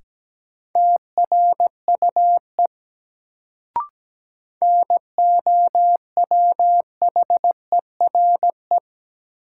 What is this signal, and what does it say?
Morse code, heard as a keyed tone, about 17 wpm, 700 Hz. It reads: TRUE NOWHERE
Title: Sets of 5 Words Encore - New Word Drill - 17wpm